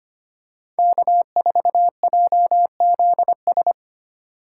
Morse code, heard as a keyed tone, about 25 wpm, 700 Hz. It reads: K4JZS
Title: USA Call Signs - 5 Characters - 2x Character Spacing - 25wpm